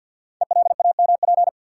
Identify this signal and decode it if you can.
Morse code, heard as a keyed tone, about 48 wpm, 700 Hz. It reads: ELAMP
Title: Code Groups - 5 Letters - 50wpm